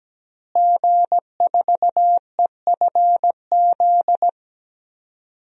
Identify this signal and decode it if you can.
Morse code, heard as a keyed tone, about 17 wpm, 700 Hz. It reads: G4EFZ
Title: DXCC-100 Call Signs - 5 Characters - Repeated 2x - 17wpm